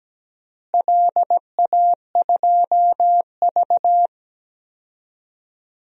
Morse code, heard as a keyed tone, about 17 wpm, 700 Hz. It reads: LA2V
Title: DXCC-100 Call Signs - 4 Characters - Repeated 3x - 17wpm